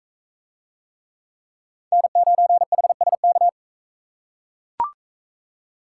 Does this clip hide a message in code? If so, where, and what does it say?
Yes: Morse code from 1.9 to 3.5 s, heard as a keyed tone, about 42 wpm, 700 Hz. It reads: N9HSK